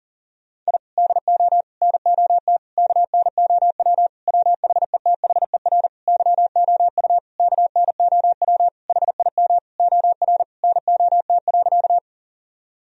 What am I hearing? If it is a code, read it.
Morse code, heard as a keyed tone, about 37 wpm, 700 Hz. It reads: I DO NOT KNOW WHETHER YOU KNOW HIM OR NOT.
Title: Sentences from Top 1000 Words - Review Part 3 - 40wpm